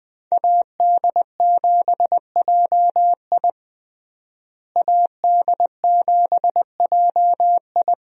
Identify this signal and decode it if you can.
Morse code, heard as a keyed tone, about 21 wpm, 700 Hz. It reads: AD7JI AD7JI
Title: USA Call Signs - 5 Characters - Repeated 2x - 20wpm